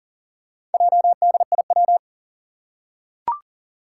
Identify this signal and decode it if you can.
Morse code, heard as a keyed tone, about 38 wpm, 700 Hz. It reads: JDIW
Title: Four Challenging Letter-Number Combinations 40wpm